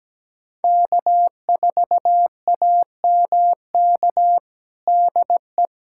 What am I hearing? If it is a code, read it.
Morse code, heard as a keyed tone, about 17 wpm, 700 Hz. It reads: K4AMK DE
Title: USA Callsign DE Callsign - 5 Characters - 17wpm